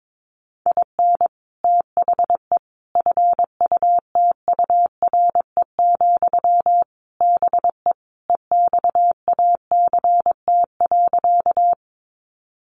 Morse code, heard as a keyed tone, about 22 wpm, 700 Hz. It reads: IN THE FUTURE, BE EXACT.
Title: Sentences from Top 1300 Words 22wpm